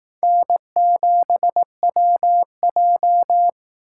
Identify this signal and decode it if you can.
Morse code, heard as a keyed tone, about 18 wpm, 700 Hz. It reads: N7WJ
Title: Speed Racing - USA 1x2 Call Signs 15wpm